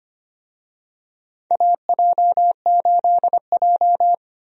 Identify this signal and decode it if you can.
Morse code, heard as a keyed tone, about 25 wpm, 700 Hz. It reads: AJ8J
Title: USA Call Signs - 4 Characters - 25wpm